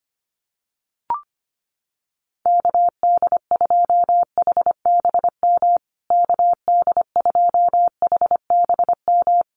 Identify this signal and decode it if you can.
Morse code, heard as a keyed tone, about 25 wpm, 700 Hz. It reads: KD2HBM KD2HBM
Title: USA Call Signs Repeated 3x 25wpm